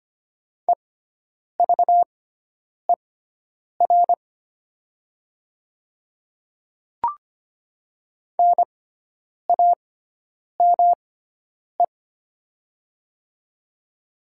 Morse code, heard as a keyed tone, about 25 wpm, spaced as for 9 wpm, 700 Hz. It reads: EVER NAME
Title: 6x Character Spacing - Top 500 Words - 25wpm